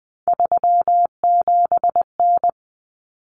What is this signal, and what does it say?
Morse code, heard as a keyed tone, about 20 wpm, 700 Hz. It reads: 37N
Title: Common Cut Numbers - 3-Digit 20wpm